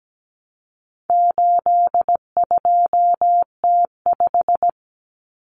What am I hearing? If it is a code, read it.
Morse code, heard as a keyed tone, about 17 wpm, 700 Hz. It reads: 82T5
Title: Common Cut Numbers - 4-Digit 17wpm